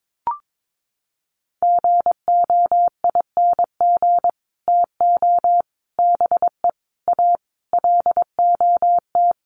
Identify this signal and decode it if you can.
Morse code, heard as a keyed tone, about 22 wpm, 700 Hz. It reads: GOING TO BE A LOT
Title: Sets of 5 Words Encore 22wpm